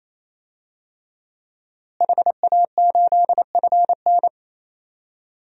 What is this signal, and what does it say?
Morse code, heard as a keyed tone, about 28 wpm, 700 Hz. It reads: HA8FN
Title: DXCC-100 Call Signs - 5 Characters - Repeated 3x - 28wpm